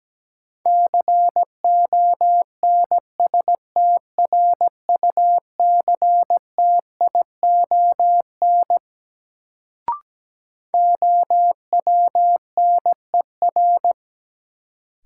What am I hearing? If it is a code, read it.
Morse code, heard as a keyed tone, about 17 wpm, 700 Hz. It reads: CONSTRUCTION OWNER
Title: Sentences from Top 1600 Words - New Word Drill Alt - 17wpm